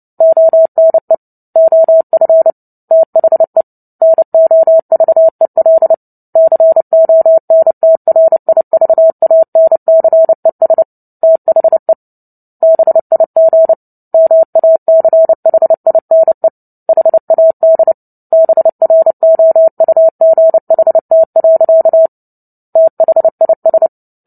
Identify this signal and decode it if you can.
Morse code, heard as a keyed tone, about 30 wpm, 650 Hz. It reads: ONE OF THE NOVEL CONTRIVANCES THE BIG MACHINE HAD BROUGHT. THIS
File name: Book Two / 03-Chapter 3 - 29 WPM